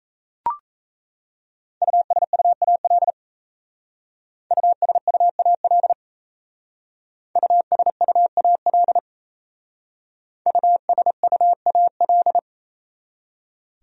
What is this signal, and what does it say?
Morse code, heard as a keyed tone, about 37 wpm, 700 Hz. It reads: USUAL USUAL USUAL USUAL